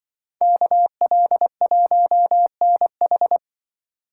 Morse code, heard as a keyed tone, about 25 wpm, 700 Hz. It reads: KL1NH
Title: Speed Racing - USA 2x2 Call Signs 20wpm